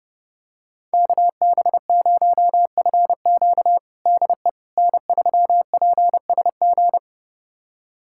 Morse code, heard as a keyed tone, about 30 wpm, 700 Hz. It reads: KB0FQ DE N3PSG
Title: USA Callsign DE Callsign - 5 Characters - 30wpm